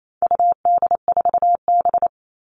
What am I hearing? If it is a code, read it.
Morse code, heard as a keyed tone, about 28 wpm, 700 Hz. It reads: UD4B
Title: All Cut Numbers - 4-Digit 28wpm